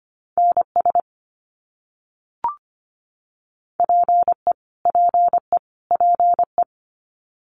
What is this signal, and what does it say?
Morse code, heard as a keyed tone, about 25 wpm, 700 Hz. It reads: NS PE PE PE
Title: Canadian Provinces and Territories - Repeated 3x - 25wpm